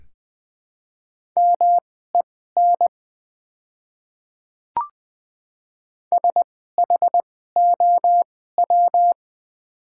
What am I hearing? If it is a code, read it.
Morse code, heard as a keyed tone, about 20 wpm, spaced as for 14 wpm, 700 Hz. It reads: MEN SHOW
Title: 2x Character Spacing - Top 300 Words - 20wpm